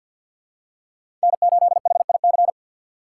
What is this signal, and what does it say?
Morse code, heard as a keyed tone, about 51 wpm, 700 Hz. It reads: N8HIC
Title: USA Call Signs - 6x Character Spacing - 50wpm